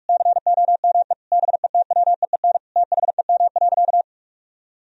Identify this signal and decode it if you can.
Morse code, heard as a keyed tone, about 46 wpm, 700 Hz. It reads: COME BETWEEN THEM.
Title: Sentences from Top 1000 Words - Review Part 1 - 45wpm